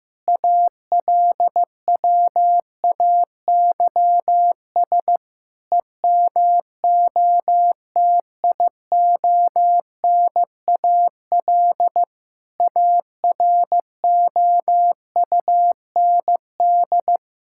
Morse code, heard as a keyed tone, about 15 wpm, 700 Hz. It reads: ALWAYS EMOTIONAL AROUND
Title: Sentences from Top 1000 Words - Review Part 5 - 15wpm